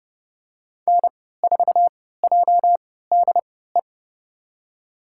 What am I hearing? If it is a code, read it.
Morse code, heard as a keyed tone, about 30 wpm, spaced as for 17 wpm, 700 Hz. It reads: N4JDE
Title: USA Call Signs - 5 Characters - 3x Character Spacing - 30wpm